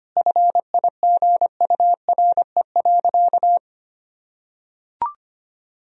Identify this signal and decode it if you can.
Morse code, heard as a keyed tone, about 25 wpm, 700 Hz. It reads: FIGURE.
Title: Sentences from Top 1000 Words - Review Part 2 - 25wpm